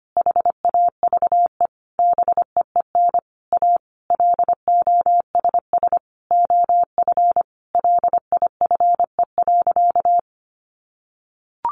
Morse code, heard as a keyed tone, about 25 wpm, 700 Hz. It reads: HAVE BEEN A LOSS OF LIFE.
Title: Sentences from Top 1400 Words 25wpm